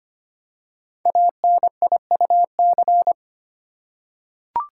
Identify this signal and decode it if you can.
Morse code, heard as a keyed tone, about 25 wpm, 700 Hz. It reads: ANIUC
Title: Code Groups - 5 Letter:Numbers - 25wpm